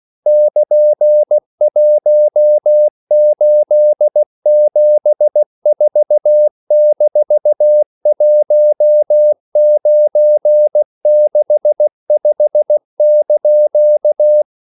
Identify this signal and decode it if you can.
Morse code, heard as a keyed tone, about 16 wpm, 600 Hz. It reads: <KN>1874-1965)
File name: CW / Practice / EnglishQuotations0416